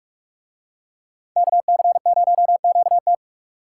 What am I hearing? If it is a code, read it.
Morse code, heard as a keyed tone, about 44 wpm, 700 Hz. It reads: KC0QT